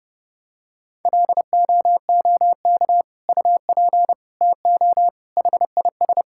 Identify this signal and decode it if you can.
Morse code, heard as a keyed tone, about 30 wpm, 700 Hz. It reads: LOOK UP TO HIS
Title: Sentences from Top 1600 Words 30wpm